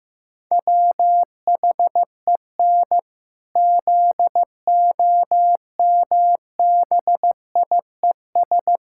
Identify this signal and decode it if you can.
Morse code, heard as a keyed tone, about 15 wpm, 700 Hz. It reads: WHEN ZOMBIES
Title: Pangram Challenge v2 - 15wpm